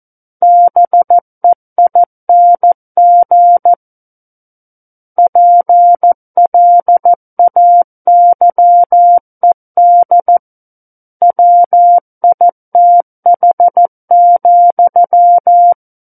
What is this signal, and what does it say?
Morse code, heard as a keyed tone, about 14 wpm, 700 Hz. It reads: BEING PLAYED WITH,
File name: Book One / 05-Chapter 5 - 14 WPM